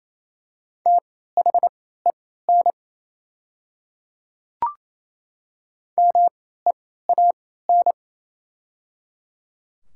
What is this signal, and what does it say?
Morse code, heard as a keyed tone, about 28 wpm, spaced as for 16 wpm, 700 Hz. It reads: THEN MEAN
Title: 3x Character Spacing - Top 300 Words - 28wpm